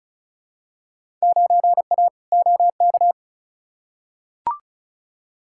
Morse code, heard as a keyed tone, about 36 wpm, 700 Hz. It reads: 9A OK
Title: Field Day Exchanges Encore - 35wpm